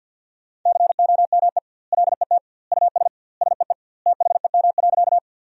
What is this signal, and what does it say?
Morse code, heard as a keyed tone, about 48 wpm, 700 Hz. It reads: COME LET US SEE THEM.
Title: 2x Word-Spacing - Sentences from Top 300 Words - 50wpm